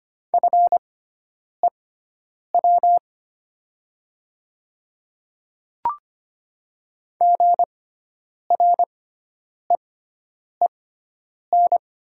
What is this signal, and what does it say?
Morse code, heard as a keyed tone, about 25 wpm, spaced as for 9 wpm, 700 Hz. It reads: FEW GREEN
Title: 6x Character Spacing - Top 500 Words - 25wpm